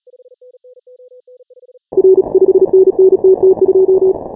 Morse code, heard as a keyed tone, about 36 wpm, 340 Hz. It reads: R 5NN TT2
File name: final / W1UJ-168-140-MorseRunner